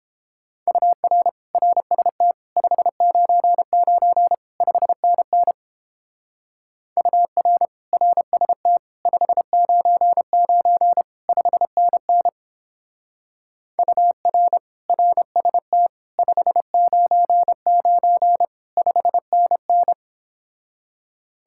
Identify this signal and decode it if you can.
Morse code, heard as a keyed tone, about 29 wpm, 700 Hz. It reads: UR RST 599 5NN UR RST 599 5NN UR RST 599 5NN